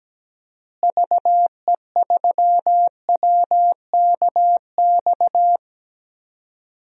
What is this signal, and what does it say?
Morse code, heard as a keyed tone, about 17 wpm, 700 Hz. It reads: VE3WKX
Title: DXCC-100 Call Signs - 6 Characters - Repeated 3x - 17wpm